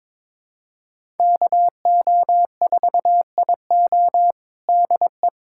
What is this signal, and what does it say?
Morse code, heard as a keyed tone, about 22 wpm, 700 Hz. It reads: KO4IO DE
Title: USA Callsign DE Callsign 22wpm